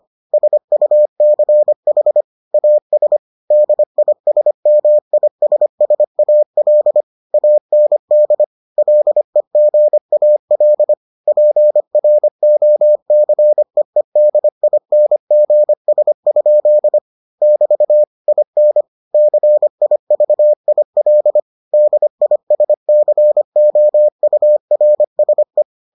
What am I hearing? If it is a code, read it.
Morse code, heard as a keyed tone, about 25 wpm, 600 Hz. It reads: SUCH AS DISMISSAL AND LEGAL PROCEEDINGS? <BT> IN CIVIL DISCOURSE